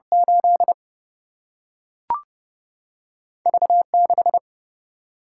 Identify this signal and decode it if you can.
Morse code, heard as a keyed tone, about 30 wpm, 700 Hz. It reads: 8 V 6
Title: Single-Letter Single-Number 30wpm